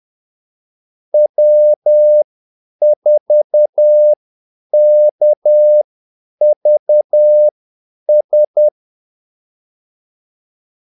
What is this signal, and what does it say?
Morse code, heard as a keyed tone, about 10 wpm, spaced as for 8 wpm, 600 Hz. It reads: W4KVS
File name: Callsigns-10-8wpm